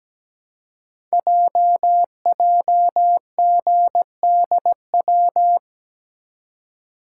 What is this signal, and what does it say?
Morse code, heard as a keyed tone, about 17 wpm, 700 Hz. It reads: JJGDW
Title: Code Groups - 5 Letters - 17wpm